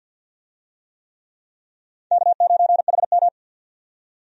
Morse code, heard as a keyed tone, about 51 wpm, 700 Hz. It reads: K9HM
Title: USA Call Signs - 4 Characters - 50wpm